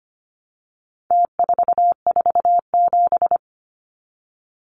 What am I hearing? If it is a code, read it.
Morse code, heard as a keyed tone, about 25 wpm, 700 Hz. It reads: T447
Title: Numbers - Contest Style - 4-Digit 25wpm